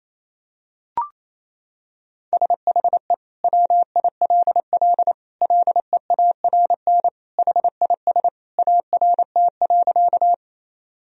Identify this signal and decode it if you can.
Morse code, heard as a keyed tone, about 28 wpm, 700 Hz. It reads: SHE WILL LEARN HIS ART.
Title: Sentences from Top 1000 Words - Review Part 3 - 28wpm